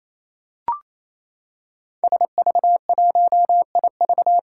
Morse code, heard as a keyed tone, about 27 wpm, 700 Hz. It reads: SV1IV
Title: Five Challenging Letter-Number Combinations 28wpm